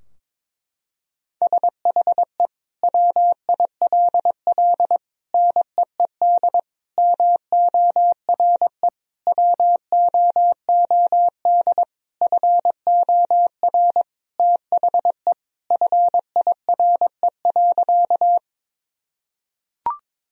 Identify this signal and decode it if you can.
Morse code, heard as a keyed tone, about 22 wpm, 700 Hz. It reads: SHE WILL NEED MORE WOOD FOR THE FIRE.
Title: Sentences from Top 1000 Words - Review Part 2 - 22wpm